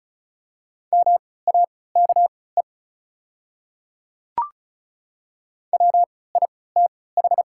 Morse code, heard as a keyed tone, about 34 wpm, spaced as for 20 wpm, 690 Hz. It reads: MAKE WITH
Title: 3x Character Spacing - Top 100 Words - 35wpm